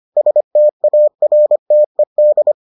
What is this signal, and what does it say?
Morse code, heard as a keyed tone, about 25 wpm, 600 Hz. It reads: STARTED